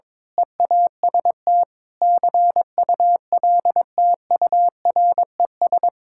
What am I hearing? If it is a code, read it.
Morse code, heard as a keyed tone, about 22 wpm, 700 Hz. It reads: EAST CULTURES
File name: practice / 22wpm / Sentences with 4-Word Phrases - 22wpm